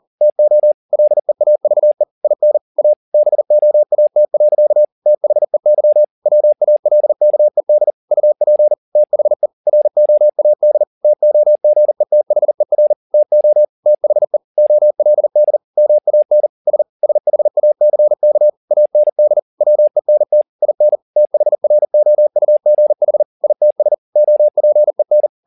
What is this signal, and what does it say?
Morse code, heard as a keyed tone, about 40 wpm, 600 Hz. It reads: TO LEAVE IN A BOAT. THEY WALKED UP THE ROAD TOGETHER TO THE OLD MAN S SHACK AND WENT IN THROUGH ITS OPEN